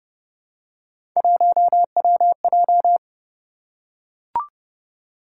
Morse code, heard as a keyed tone, about 29 wpm, 700 Hz. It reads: 1WJ